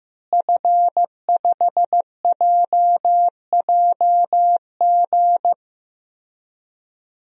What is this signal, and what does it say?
Morse code, heard as a keyed tone, about 15 wpm, 700 Hz. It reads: F5JJG
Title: DXCC-100 Call Signs - 5 Characters - Repeated 2x - 15wpm